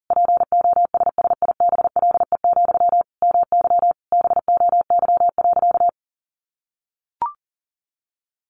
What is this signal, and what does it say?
Morse code, heard as a keyed tone, about 40 wpm, 700 Hz. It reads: POSSIBLE, MY BOY.
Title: Sentences from Top 800 Words 40wpm